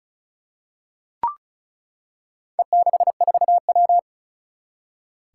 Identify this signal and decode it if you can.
Morse code, heard as a keyed tone, about 36 wpm, 700 Hz. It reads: E64W
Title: Four Challenging Letter-Number Combinations 35wpm